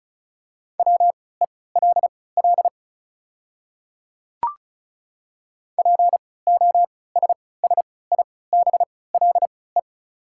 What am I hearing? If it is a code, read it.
Morse code, heard as a keyed tone, about 36 wpm, spaced as for 20 wpm, 690 Hz. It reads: WELL POSSIBLE
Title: 3x Character Spacing - Top 500 Words - 35wpm